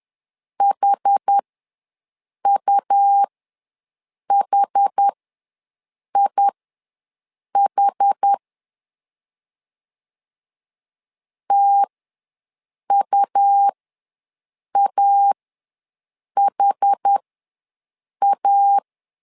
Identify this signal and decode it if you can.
Morse code, heard as a keyed tone, about 10 wpm, spaced as for 6 wpm, 800 Hz. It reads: HUHIH TUAHA
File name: LEZIONI / 2 UHDN / 2 UH ws 10wpm 2029